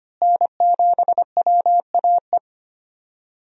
Morse code, heard as a keyed tone, about 25 wpm, 700 Hz. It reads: N7WAE
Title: USA Call Signs - 5 Characters - Repeated 2x - 25wpm